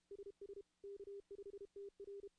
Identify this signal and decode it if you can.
Morse code, heard as a keyed tone, about 32 wpm, 400 Hz. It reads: SS K5TR
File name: contests / CW-SS-k5tr- -0000